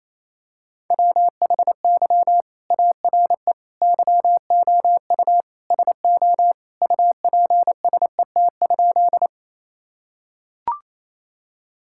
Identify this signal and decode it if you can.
Morse code, heard as a keyed tone, about 27 wpm, 700 Hz. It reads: WHY ARE YOU SO UPSET?
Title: Sentences from Top 1000 Words - Review Part 5 - 28wpm